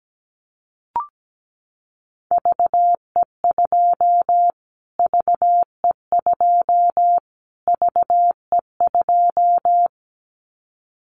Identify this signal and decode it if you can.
Morse code, heard as a keyed tone, about 17 wpm, 700 Hz. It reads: VE2 VE2 VE2
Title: Callsign Prefixes for Top 25 DXCC - Repeated 3x - 17wpm